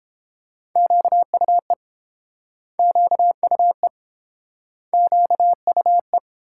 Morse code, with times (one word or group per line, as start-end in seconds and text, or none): 0.8-1.7 QUE
2.8-3.9 QUE
4.9-6.2 QUE